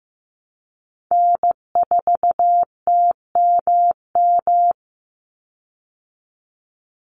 Morse code, heard as a keyed tone, about 15 wpm, 700 Hz. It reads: N4TMM
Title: Call Signs Repeated 3x - 15wpm